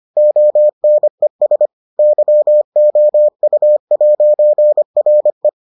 Morse code, heard as a keyed tone, about 25 wpm, 600 Hz. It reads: ONES YOU'RE